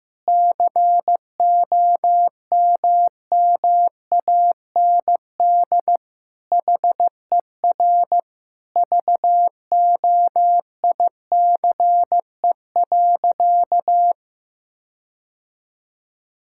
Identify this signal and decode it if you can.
Morse code, heard as a keyed tone, about 15 wpm, 700 Hz. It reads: COMMAND HER VOICE.
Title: Sentences from Top 1600 Words 15wpm